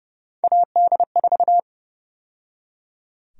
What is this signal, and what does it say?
Morse code, heard as a keyed tone, about 30 wpm, 700 Hz. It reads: AD4
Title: USA Call Sign Prefixes - 30wpm